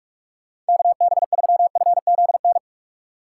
Morse code, heard as a keyed tone, about 62 wpm, spaced as for 53 wpm, 700 Hz. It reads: KB3FZN
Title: Speed Racing - USA 2x3 Call Signs 45wpm